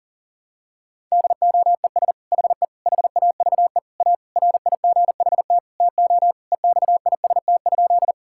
Morse code, heard as a keyed tone, about 40 wpm, 700 Hz. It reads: DOES HE HAVE A RIGHT TO EXIST?
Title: Sentences from Top 1000 Words - Review Part 5 - 40wpm